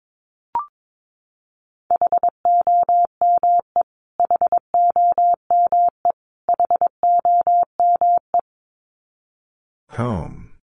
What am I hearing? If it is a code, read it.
Morse code, heard as a keyed tone, about 22 wpm, 700 Hz. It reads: HOME HOME HOME
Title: Top 300 Words Repeated 22wpm